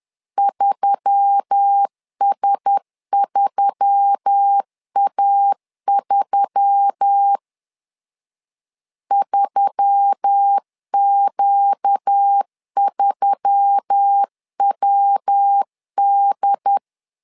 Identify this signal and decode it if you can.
Morse code, heard as a keyed tone, about 10 wpm, 800 Hz. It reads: 3S3A3 3Q3WD
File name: LEZIONI / 4 Q1S3 10wpm 2427